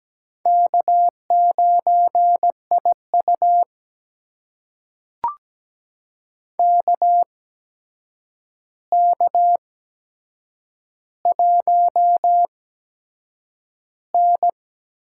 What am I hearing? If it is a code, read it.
Morse code, heard as a keyed tone, about 17 wpm, spaced as for 7 wpm, 700 Hz. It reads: K9IU KK1N
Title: USA Call Signs - 4 Characters - 8x Character Spacing - 17wpm